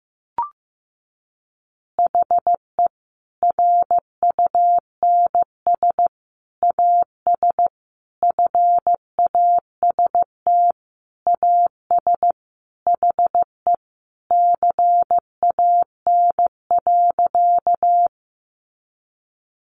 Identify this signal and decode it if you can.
Morse code, heard as a keyed tone, about 15 wpm, 700 Hz. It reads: HE RUNS AS FAST AS HE CAN.
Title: Additional Easy Phrases 15wpm